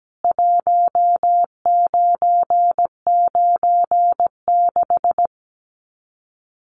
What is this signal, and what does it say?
Morse code, heard as a keyed tone, about 17 wpm, 700 Hz. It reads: 1996
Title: Common QSO Phrases 17wpm